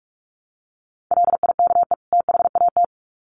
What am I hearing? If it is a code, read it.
Morse code, heard as a keyed tone, about 44 wpm, 700 Hz. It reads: LIKE THAT
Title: Sets of 2 Words 45wpm